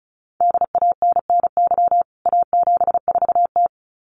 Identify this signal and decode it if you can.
Morse code, heard as a keyed tone, about 34 wpm, 700 Hz. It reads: DANNY A74T
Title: CWT Exchange 35wpm